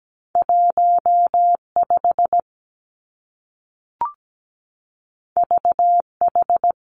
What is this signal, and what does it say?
Morse code, heard as a keyed tone, about 17 wpm, 700 Hz. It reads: 1 5 V H
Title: Two Challenging Letter-Number Combinations 17wpm